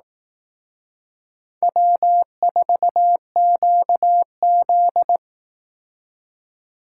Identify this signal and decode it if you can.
Morse code, heard as a keyed tone, about 18 wpm, 700 Hz. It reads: W4QZ
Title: Speed Racing - USA Call Signs 15wpm